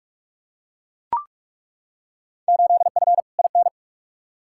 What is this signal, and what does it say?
Morse code, heard as a keyed tone, about 42 wpm, 700 Hz. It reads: 8F IN
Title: Field Day Exchanges Encore - 45wpm